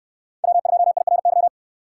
Morse code, heard as a keyed tone, about 68 wpm, 700 Hz. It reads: W2IRY